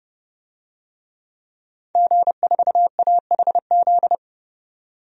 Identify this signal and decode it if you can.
Morse code, heard as a keyed tone, about 31 wpm, 700 Hz. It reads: G4AHZ
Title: DXCC-100 Call Signs - 5 Characters - 30wpm